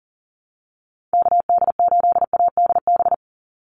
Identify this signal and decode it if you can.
Morse code, heard as a keyed tone, about 40 wpm, 700 Hz. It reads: KD8ADB
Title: USA Call Signs 40wpm